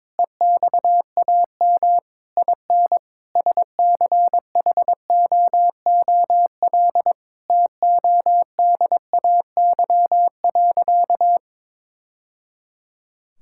Morse code, heard as a keyed tone, about 22 wpm, 700 Hz. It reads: EXAM IN SCHOOL TODAY.
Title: Sentences from Top 1000 Words - Review Part 5 - 22wpm